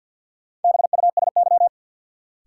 Morse code, heard as a keyed tone, about 53 wpm, 700 Hz. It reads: BUSY
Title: Sentences from Top 1500 Words - New Word Drill Alt - 50wpm